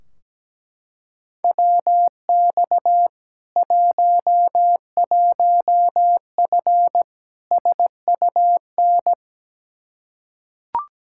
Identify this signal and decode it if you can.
Morse code, heard as a keyed tone, about 17 wpm, 700 Hz. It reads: WX 11F SUN
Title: Practice QSO - Exchange 2 - Info - 17wpm